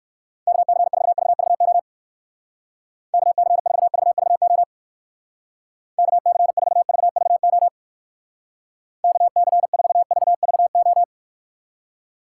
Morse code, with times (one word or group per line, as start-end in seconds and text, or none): 0.5-1.8 KC4VVO
3.1-4.6 KC4VVO
6.0-7.7 KC4VVO
9.0-11.0 KC4VVO